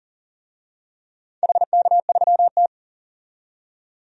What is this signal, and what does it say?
Morse code, heard as a keyed tone, about 40 wpm, 700 Hz. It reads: HK3T
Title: DXCC-100 Call Signs - 4 Characters - 40wpm